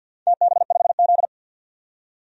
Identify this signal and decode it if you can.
Morse code, heard as a keyed tone, about 48 wpm, 700 Hz. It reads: TB5Z
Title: DXCC-100 Call Signs - 4 Characters - 50wpm